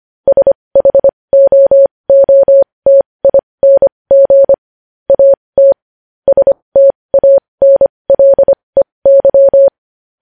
Morse code, heard as a keyed tone, about 25 wpm, spaced as for 20 wpm, 550 Hz. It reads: SHOOTING AT STANLEY
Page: news headlines in morse code at 20 wpm